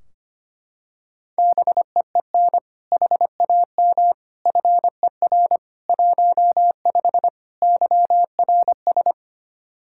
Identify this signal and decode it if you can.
Morse code, heard as a keyed tone, about 25 wpm, 700 Hz. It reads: BEEN HAM FER 15 YRS
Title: Practice QSO - Exchange 3 - More Info - 25wpm